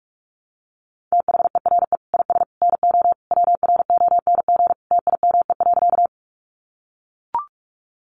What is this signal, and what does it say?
Morse code, heard as a keyed tone, about 44 wpm, 700 Hz. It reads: THERE IS NO WRONG TIME.
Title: Sentences from Top 900 Words 45wpm